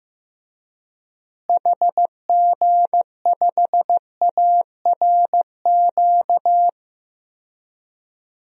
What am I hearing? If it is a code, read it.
Morse code, heard as a keyed tone, about 15 wpm, 700 Hz. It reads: HG5ARQ
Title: DXCC-100 Call Signs - 6 Characters - Repeated 2x - 15wpm